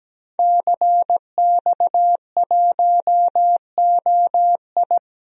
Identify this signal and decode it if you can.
Morse code, heard as a keyed tone, about 17 wpm, 700 Hz. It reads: CX1OI
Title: DXCC-100 Call Signs - 5 Characters - Repeated 2x - 17wpm